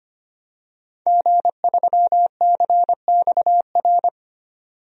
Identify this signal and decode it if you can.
Morse code, heard as a keyed tone, about 25 wpm, 690 Hz. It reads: G3CXR
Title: DXCC-100 Call Signs - 5 Characters - 25wpm